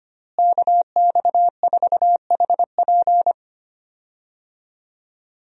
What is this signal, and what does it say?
Morse code, heard as a keyed tone, about 25 wpm, 700 Hz. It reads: KX4HP
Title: USA Call Signs - 5 Characters - Repeated 3x - 25wpm